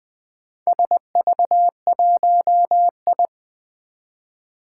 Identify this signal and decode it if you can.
Morse code, heard as a keyed tone, about 20 wpm, 700 Hz. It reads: SV1I
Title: Four Challenging Letter-Number Combinations 20wpm